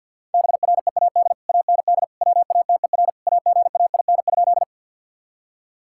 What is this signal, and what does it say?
Morse code, heard as a keyed tone, about 50 wpm, 700 Hz. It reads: BREAD AND WATER AGAIN?
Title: Sentences from Top 1600 Words 50wpm